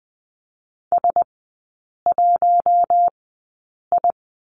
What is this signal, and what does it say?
Morse code, heard as a keyed tone, about 20 wpm, spaced as for 8 wpm, 700 Hz. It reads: S1I
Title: 047 Lesson 9e - Number 4 - Three Characters 20wpm